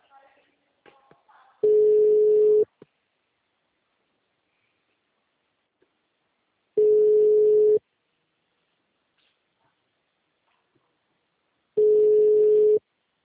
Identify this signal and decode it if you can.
Ringback tone (Europe, 425 Hz), with speech.
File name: out- -2002- - - .1444